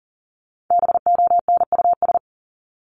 Morse code, heard as a keyed tone, about 40 wpm, 700 Hz. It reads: BONUS
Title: Top 1751-2000 Words 40wpm